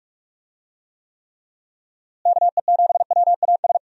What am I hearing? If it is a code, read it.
Morse code, heard as a keyed tone, about 46 wpm, 700 Hz. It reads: KE7WAS
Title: USA Call Signs - 6 Characters - Repeated 3x - 45wpm